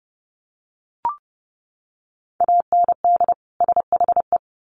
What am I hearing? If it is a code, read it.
Morse code, heard as a keyed tone, about 30 wpm, 700 Hz. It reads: AND SHE
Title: Sets of 2 Words 30wpm